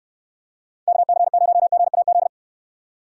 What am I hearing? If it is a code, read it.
Morse code, heard as a keyed tone, about 66 wpm, 700 Hz. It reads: KC0GAZ